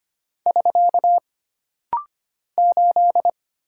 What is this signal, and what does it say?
Morse code, heard as a keyed tone, about 25 wpm, 700 Hz. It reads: <SK> 8
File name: Single Character (Letter:Number:Punctuation:Prosign) - Rapid-Fire - 25wpm